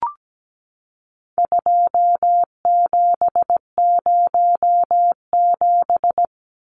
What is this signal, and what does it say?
Morse code, heard as a keyed tone, about 17 wpm, 700 Hz. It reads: 2707
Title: Numbers - 4-Digit 17wpm